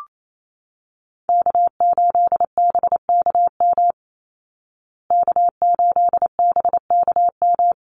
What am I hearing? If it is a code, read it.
Morse code, heard as a keyed tone, about 28 wpm, 700 Hz. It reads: K8BKM K8BKM
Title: Call Signs Repeated 2x - 28wpm